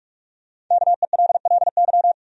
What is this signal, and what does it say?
Morse code, heard as a keyed tone, about 46 wpm, 700 Hz. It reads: KELLY
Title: USA Common Names - Max 5 Letters - 45wpm